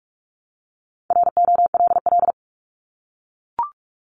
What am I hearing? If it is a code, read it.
Morse code, heard as a keyed tone, about 46 wpm, 700 Hz. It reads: ROLL
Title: Multi-letter Phonemes Repeated 3x 45wpm